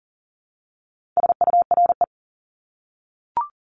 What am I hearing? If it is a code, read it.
Morse code, heard as a keyed tone, about 40 wpm, 700 Hz. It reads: SURE
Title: Top 300 Words 40wpm